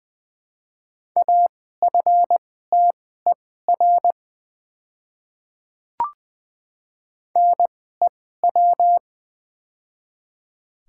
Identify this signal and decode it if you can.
Morse code, heard as a keyed tone, about 20 wpm, spaced as for 14 wpm, 700 Hz. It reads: AFTER NEW
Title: 2x Character Spacing - Top 300 Words - 20wpm